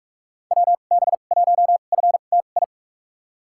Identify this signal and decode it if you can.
Morse code, heard as a keyed tone, about 43 wpm, spaced as for 32 wpm, 700 Hz. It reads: WB1FTI
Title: USA Call Signs - 6 Characters - 2x Character Spacing - 45wpm